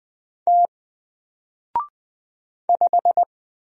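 Morse code, heard as a keyed tone, about 20 wpm, 700 Hz. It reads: T 5
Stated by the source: Single letter and number Morse drill at 20 wpm, from N4LCD.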